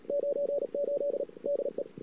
Morse code, heard as a keyed tone, about 40 wpm, 570 Hz. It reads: 98 DE
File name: CW / 2008-ss-cw-k5tr-op-n5rz -21:00